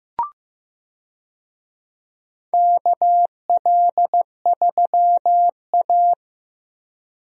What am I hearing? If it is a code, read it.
Morse code, heard as a keyed tone, about 15 wpm, 700 Hz. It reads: KL3A
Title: USA Call Signs - 4 Characters - Repeated 2x - 15wpm